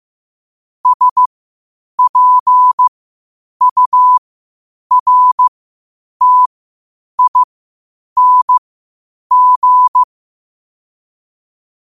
Morse code, heard as a keyed tone, about 15 wpm, spaced as for 8 wpm, 1000 Hz. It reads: SPURTING